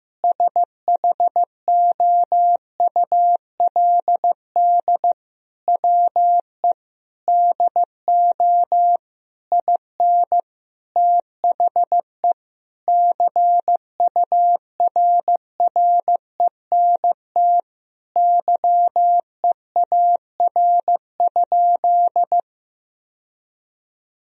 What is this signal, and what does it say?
Morse code, heard as a keyed tone, about 15 wpm, 700 Hz. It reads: SHOULD WE DO IN THE CURRENT YEAR?
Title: Sentences from Top 1000 Words - Review Part 4 - 15wpm